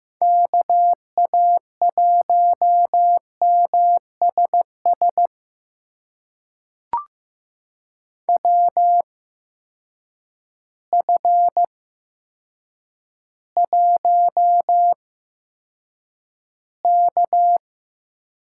Morse code, heard as a keyed tone, about 15 wpm, spaced as for 8 wpm, 700 Hz. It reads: KA1MSS WF1K
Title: USA Call Signs - 8x Character Spacing - 15wpm